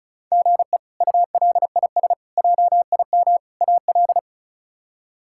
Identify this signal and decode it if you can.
Morse code, heard as a keyed tone, about 34 wpm, 700 Hz. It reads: GE ULIS JIM AL
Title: K1USN SST QSO Practice - 35wpm